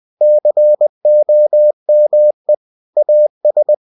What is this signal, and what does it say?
Morse code, heard as a keyed tone, about 20 wpm, 600 Hz. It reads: COME AS
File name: cw-books / winnie-the-pooh / winnie-20wpm-page-0098